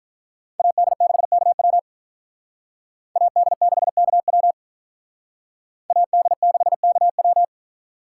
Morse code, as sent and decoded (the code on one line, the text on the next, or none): .- -.. -.... -.- .-- / .- -.. -.... -.- .-- / .- -.. -.... -.- .--
AD6KW AD6KW AD6KW